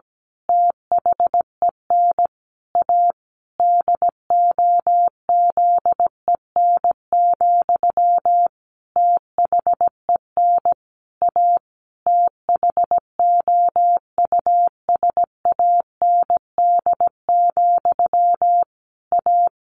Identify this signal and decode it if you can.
Morse code, heard as a keyed tone, about 17 wpm, 700 Hz. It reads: THEN A DOZEN, THEN A THOUSAND, A
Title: A Sound Of Thunder 17wpm